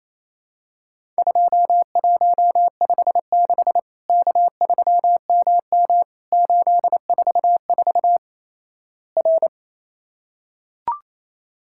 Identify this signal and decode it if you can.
Morse code, heard as a keyed tone, about 28 wpm, 700 Hz. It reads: 2156 K3MM 844 R
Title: Worked All Europe DX Contest - QTC Exchange - 28wpm